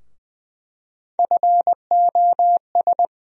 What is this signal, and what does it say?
Morse code, heard as a keyed tone, about 20 wpm, 700 Hz. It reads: FOS
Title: USA Call Sign Postfixes - 3 Letters 20wpm